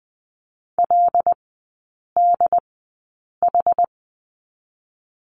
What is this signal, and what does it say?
Morse code, heard as a keyed tone, about 20 wpm, spaced as for 8 wpm, 700 Hz. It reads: LDH
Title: 100 Lesson 17e - Letter u - Three Characters 20wpm